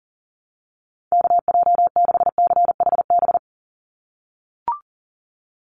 Morse code, heard as a keyed tone, about 40 wpm, 700 Hz. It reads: KJ6CHB
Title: USA Call Signs 40wpm